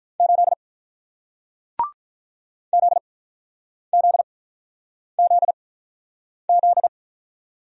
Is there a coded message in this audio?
Yes: Morse code from 0.2 to 6.9 s, heard as a keyed tone, about 48 wpm, spaced as for 9 wpm, 700 Hz. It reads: 8 ZZZZ